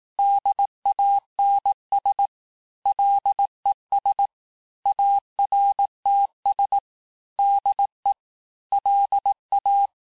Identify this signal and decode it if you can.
Morse code, heard as a keyed tone, about 18 wpm, 800 Hz. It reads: DANS LES ARTS DE LA
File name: cours-du-lundi files / cours-1594--teste-clair-vitesse-17mpm-puis-22mpm